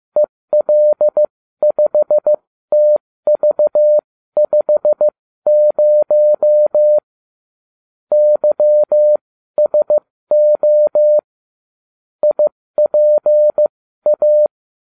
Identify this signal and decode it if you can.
Morse code, heard as a keyed tone, about 15 wpm, spaced as for 13 wpm, 600 Hz. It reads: EL5TV50 YSO IPA